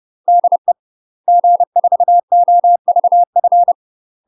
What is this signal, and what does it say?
Morse code, heard as a keyed tone, about 30 wpm, 700 Hz. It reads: DE G4OVF
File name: UK MASTER QSOs TRAINING/30WPM